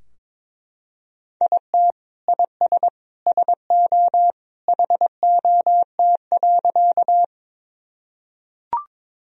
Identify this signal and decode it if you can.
Morse code, heard as a keyed tone, about 22 wpm, 700 Hz. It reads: IT IS SO HOT.
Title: Sentences from Top 1000 Words - Review Part 3 - 22wpm